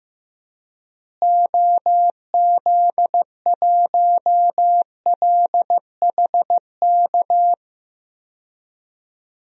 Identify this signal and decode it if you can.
Morse code, heard as a keyed tone, about 15 wpm, 700 Hz. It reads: OZ1LHK
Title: DXCC-100 Call Signs - 6 Characters - Repeated 2x - 15wpm